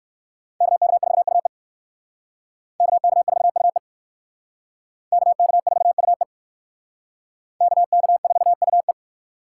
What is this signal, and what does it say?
Morse code, heard as a keyed tone, about 58 wpm, 700 Hz. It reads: KK4UE KK4UE KK4UE KK4UE